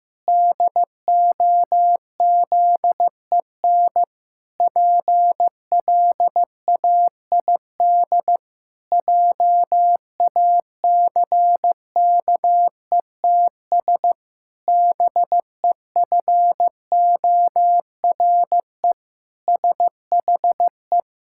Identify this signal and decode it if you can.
Morse code, heard as a keyed tone, about 15 wpm, 700 Hz. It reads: DOZEN PLAID JACKETS BEFORE SHE
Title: Pangram Challenge v2 - 15wpm